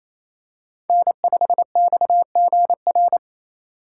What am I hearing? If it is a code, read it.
Morse code, heard as a keyed tone, about 27 wpm, 700 Hz. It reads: N5XGR
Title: USA Call Signs - 3x Character Spacing - 28wpm